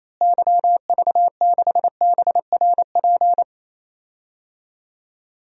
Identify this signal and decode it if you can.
Morse code, heard as a keyed tone, about 27 wpm, 700 Hz. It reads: YV6BRP
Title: DXCC-100 Call Signs - 6 Characters - Repeated 2x - 28wpm